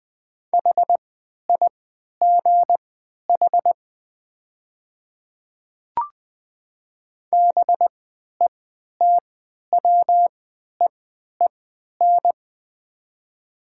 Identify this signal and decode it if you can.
Morse code, heard as a keyed tone, about 20 wpm, spaced as for 11 wpm, 700 Hz. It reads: HIGH BETWEEN
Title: 3x Character Spacing - Top 300 Words - 20wpm